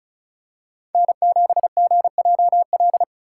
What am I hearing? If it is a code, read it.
Morse code, heard as a keyed tone, about 34 wpm, 700 Hz. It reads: N7GJL
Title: USA Call Signs - 5 Characters - 2x Character Spacing - 35wpm